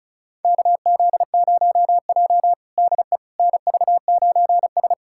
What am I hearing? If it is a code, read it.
Morse code, heard as a keyed tone, about 36 wpm, 700 Hz. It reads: KZ0J DE NV9S
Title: USA Callsign DE Callsign - 4 Characters - 35wpm